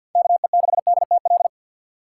Morse code, heard as a keyed tone, about 53 wpm, 690 Hz. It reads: KE6DTL
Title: USA Call Signs - 2x Character Spacing - 50wpm